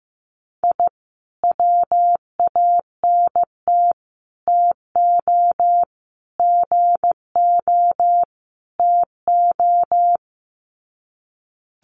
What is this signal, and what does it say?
Morse code, heard as a keyed tone, about 15 wpm, 700 Hz. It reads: I WANT TO GO TO
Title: Sets of 5 Words Encore 15wpm